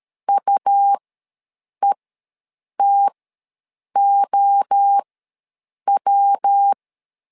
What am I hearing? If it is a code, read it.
Morse code, heard as a keyed tone, about 13 wpm, spaced as for 7 wpm, 800 Hz. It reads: UETOW
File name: LEZIONI / 3 MWRO / 3 MWRO ws 12wpm 1914